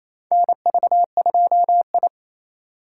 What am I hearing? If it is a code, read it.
Morse code, heard as a keyed tone, about 28 wpm, 700 Hz. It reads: NV2I
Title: USA Call Signs - 4 Characters - 6x Character Spacing - 28wpm